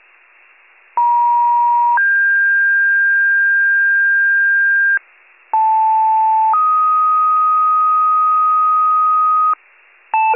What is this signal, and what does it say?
Two-tone page, heard as a one-second tone, then a three-second tone, about 950 then 1600 Hz, about 880 then 1250 Hz, with noise. It is heard from 1.0 to 9.5 s.